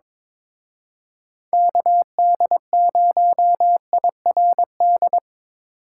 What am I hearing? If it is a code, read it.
Morse code, heard as a keyed tone, about 26 wpm, 700 Hz. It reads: KD0IRD
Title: Speed Racing - USA 2x3 Call Signs 22wpm